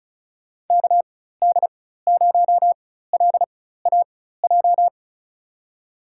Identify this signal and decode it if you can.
Morse code, heard as a keyed tone, about 36 wpm, spaced as for 16 wpm, 700 Hz. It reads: KD0LAJ
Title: USA Call Signs - 6 Characters - 4x Character Spacing - 35wpm